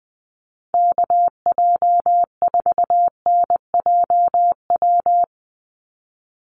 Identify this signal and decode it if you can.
Morse code, heard as a keyed tone, about 20 wpm, 700 Hz. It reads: KJ4NJW
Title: USA Call Signs 20wpm